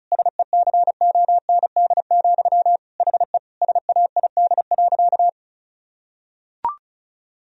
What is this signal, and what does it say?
Morse code, heard as a keyed tone, about 34 wpm, 690 Hz. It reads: SECOND, HE SAID.
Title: Sentences from Top 1000 Words - Review Part 1 - 35wpm